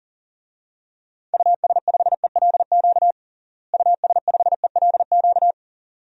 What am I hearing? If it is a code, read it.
Morse code, heard as a keyed tone, about 40 wpm, 700 Hz. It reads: US5ELQ US5ELQ
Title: DXCC-100 Call Signs - 6 Characters - Repeated 2x - 40wpm